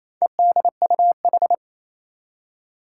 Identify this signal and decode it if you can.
Morse code, heard as a keyed tone, about 28 wpm, 700 Hz. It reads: EDUH
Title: Four Challenging Letter-Number Combinations 28wpm